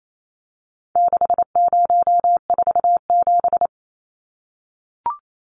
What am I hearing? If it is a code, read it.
Morse code, heard as a keyed tone, about 28 wpm, 700 Hz. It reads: 6047
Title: Numbers - 4-Digit 28wpm